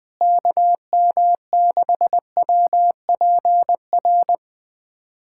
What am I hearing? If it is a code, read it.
Morse code, heard as a keyed tone, about 21 wpm, 700 Hz. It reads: KM6WPR
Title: USA Callsign DE Callsign 20wpm